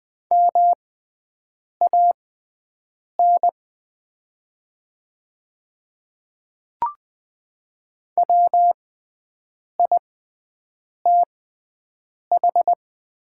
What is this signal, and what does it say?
Morse code, heard as a keyed tone, about 20 wpm, spaced as for 7 wpm, 700 Hz. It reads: MAN WITH
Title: 6x Character Spacing - Top 300 Words - 20wpm